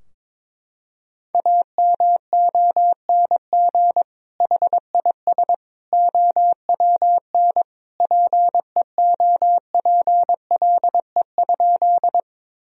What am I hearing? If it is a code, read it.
Morse code, heard as a keyed tone, about 22 wpm, 700 Hz. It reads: AMONG HIS OWN PEOPLE?
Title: Sentences from Top 1000 Words - Review Part 3 - 22wpm